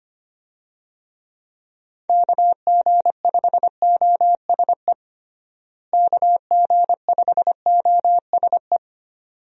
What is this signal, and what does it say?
Morse code, heard as a keyed tone, about 25 wpm, 700 Hz. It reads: KG5OSE KG5OSE
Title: USA Call Signs - 6 Characters - Repeated 2x - 25wpm